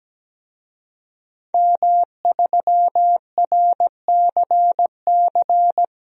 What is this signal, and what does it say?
Morse code, heard as a keyed tone, about 17 wpm, 700 Hz. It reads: M3RCC
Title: DXCC-100 Call Signs - 5 Characters - Repeated 2x - 17wpm